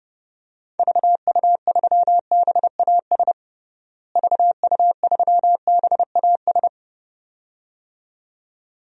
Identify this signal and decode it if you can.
Morse code, heard as a keyed tone, about 30 wpm, 700 Hz. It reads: VU3BAS VU3BAS
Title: DXCC-100 Call Signs - 6 Characters - Repeated 2x - 30wpm